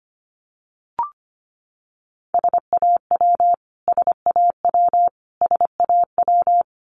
Morse code, heard as a keyed tone, about 25 wpm, 700 Hz. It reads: SAW SAW SAW
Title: Top 500 Words Repeated 25wpm